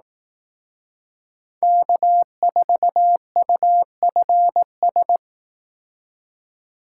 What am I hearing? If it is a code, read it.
Morse code, heard as a keyed tone, about 18 wpm, 700 Hz. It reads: K4UFS
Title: Speed Racing - USA 1x3 Call Signs 15wpm